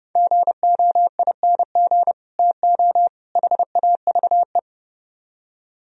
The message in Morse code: GOING TO HAVE

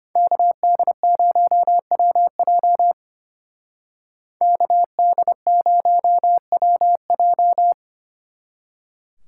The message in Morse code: KD0WJ KD0WJ

